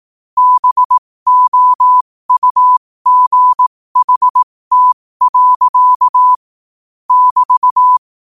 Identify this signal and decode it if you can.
Morse code, heard as a keyed tone, about 18 wpm, spaced as for 15 wpm, 1000 Hz. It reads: BOUGHT. <BT>
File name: qotdfiles / 15wpm